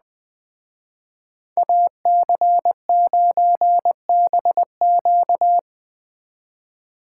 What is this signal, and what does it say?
Morse code, heard as a keyed tone, about 20 wpm, 700 Hz. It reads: AC9BQ
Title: Speed Racing - USA Call Signs 17wpm